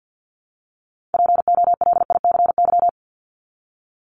Morse code, heard as a keyed tone, about 50 wpm, 700 Hz. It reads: POLICY
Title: Top 501-750 Words 50wpm